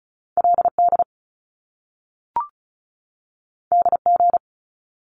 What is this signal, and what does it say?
Morse code, heard as a keyed tone, about 36 wpm, 700 Hz. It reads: LD DG
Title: Top Two Letter Combinations in Words 35wpm